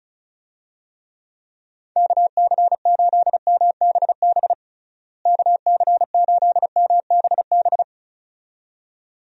Morse code, heard as a keyed tone, about 36 wpm, 700 Hz. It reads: KC8MBB KC8MBB
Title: USA Call Signs - 6 Characters - Repeated 2x - 35wpm